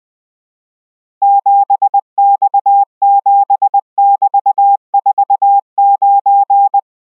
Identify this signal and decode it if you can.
Morse code, heard as a keyed tone, about 20 wpm, 800 Hz. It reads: 7X7<BT>49